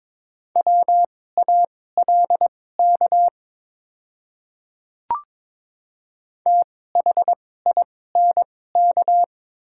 Morse code, heard as a keyed tone, about 22 wpm, spaced as for 16 wpm, 700 Hz. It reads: WALK THINK
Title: 2x Character Spacing - Top 300 Words - 22wpm